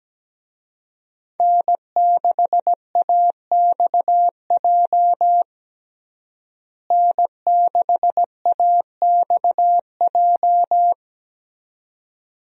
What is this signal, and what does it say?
Morse code, heard as a keyed tone, about 17 wpm, 700 Hz. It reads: N6AXJ N6AXJ
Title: USA Call Signs - 5 Characters - Repeated 2x - 17wpm